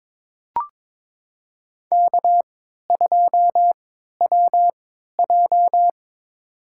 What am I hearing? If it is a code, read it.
Morse code, heard as a keyed tone, about 22 wpm, spaced as for 12 wpm, 700 Hz. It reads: K2WJ
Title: USA Call Signs - 4 Characters - 3x Character Spacing - 22wpm